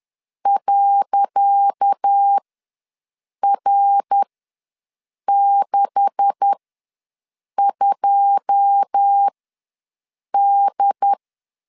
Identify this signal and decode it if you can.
Morse code, heard as a keyed tone, about 10 wpm, spaced as for 6 wpm, 800 Hz. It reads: .R62D
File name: LEZIONI / 10 SIGNS / 10C DOT ws 10wpm 4213